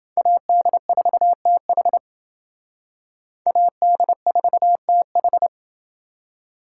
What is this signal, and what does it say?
Morse code, heard as a keyed tone, about 28 wpm, 700 Hz. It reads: AD4TH AD4TH